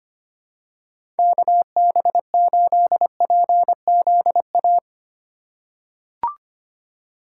Morse code, heard as a keyed tone, about 25 wpm, 700 Hz. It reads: KB8PZA
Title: USA Call Signs - 6 Characters - 3x Character Spacing - 25wpm